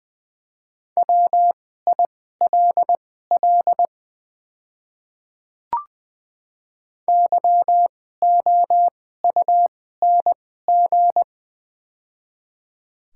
Morse code, heard as a keyed tone, about 20 wpm, spaced as for 14 wpm, 700 Hz. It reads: WILL YOUNG
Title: 2x Character Spacing - Top 500 Words - 20wpm